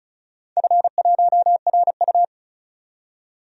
Morse code, heard as a keyed tone, about 36 wpm, 690 Hz. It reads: F1RU
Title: DXCC-100 Call Signs - 4 Characters - 35wpm